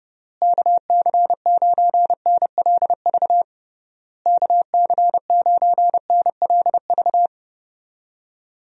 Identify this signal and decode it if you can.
Morse code, heard as a keyed tone, about 32 wpm, 700 Hz. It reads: KC9NLV KC9NLV
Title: USA Call Signs - 6 Characters - Repeated 3x - 30wpm